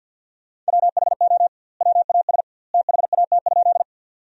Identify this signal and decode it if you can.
Morse code, heard as a keyed tone, about 50 wpm, 700 Hz. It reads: WHO WAS THAT?
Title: 2x Word-Spacing - Sentences from Top 100 Words - 50wpm